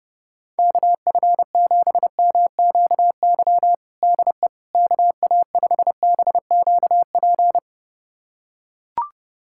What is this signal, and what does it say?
Morse code, heard as a keyed tone, about 30 wpm, 700 Hz. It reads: KF7MQY DE KA5BQP
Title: USA Callsign DE Callsign - 6 Characters - 30wpm